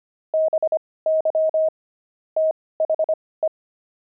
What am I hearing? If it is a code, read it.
Morse code, heard as a keyed tone, about 25 wpm, spaced as for 18 wpm, 630 Hz. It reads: BY THE
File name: pangram-fs-18-cs-25